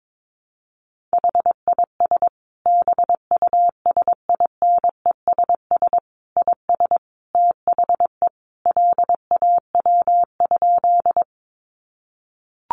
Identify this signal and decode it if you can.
Morse code, heard as a keyed tone, about 22 wpm, 700 Hz. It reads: HIS BUSINESS IS THE LAW?
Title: Sentences from Top 600 Words 22wpm